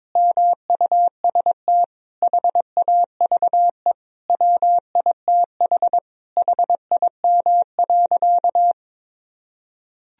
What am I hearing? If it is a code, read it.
Morse code, heard as a keyed tone, about 22 wpm, 700 Hz. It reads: MUST HAVE WITH HIM.
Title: Sentences from Top 1000 Words - Review Part 1 - 22wpm